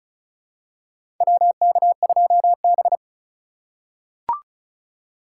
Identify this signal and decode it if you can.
Morse code, heard as a keyed tone, about 36 wpm, 700 Hz. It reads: WK2B
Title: USA Call Signs - 4x Character Spacing - 35wpm